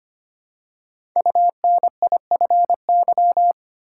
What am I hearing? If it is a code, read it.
Morse code, heard as a keyed tone, about 25 wpm, 700 Hz. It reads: UNIFY
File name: 5 Letter Words - First 3 Letters Given - 25wpm